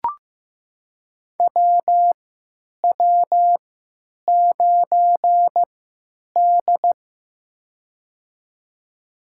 Morse code, heard as a keyed tone, about 15 wpm, spaced as for 9 wpm, 700 Hz. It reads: WW9D
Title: USA Call Signs - 3x Character Spacing - 15wpm